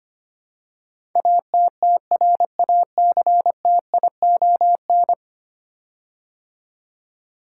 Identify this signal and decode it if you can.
Morse code, heard as a keyed tone, about 25 wpm, 700 Hz. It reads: ATTRACTION
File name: 10 Letter Words - First 3 Letters Given - 25wpm